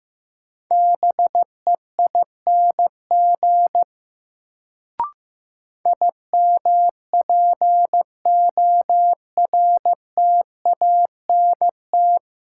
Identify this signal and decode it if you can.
Morse code, heard as a keyed tone, about 15 wpm, 700 Hz. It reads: BEING IMPORTANT
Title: Sentences from Top 600 Words - New Word Drill - 15wpm